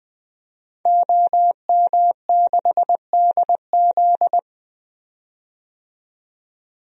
Morse code, heard as a keyed tone, about 21 wpm, 700 Hz. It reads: OM6DZ
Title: DXCC-100 Call Signs - 5 Characters - 20wpm